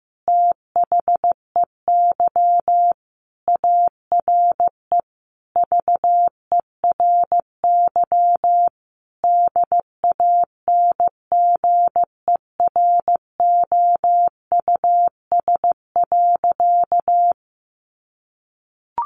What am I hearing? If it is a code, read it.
Morse code, heard as a keyed tone, about 15 wpm, 700 Hz. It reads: THEY ARE VERY DANGEROUS.
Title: Sentences from Top 1200 Words 15wpm